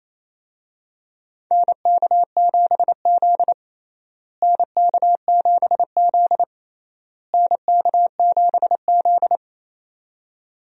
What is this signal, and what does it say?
Morse code, heard as a keyed tone, about 28 wpm, 700 Hz. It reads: NK7Z NK7Z NK7Z
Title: USA Call Signs - 4 Characters - Repeated 3x - 28wpm